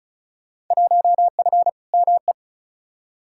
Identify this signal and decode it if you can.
Morse code, heard as a keyed tone, about 34 wpm, 700 Hz. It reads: 1F ME
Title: Field Day Exchanges Encore - 35wpm